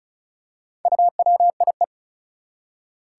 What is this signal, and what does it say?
Morse code, heard as a keyed tone, about 34 wpm, 700 Hz. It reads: UWIE
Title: Four Challenging Letter-Number Combinations 35wpm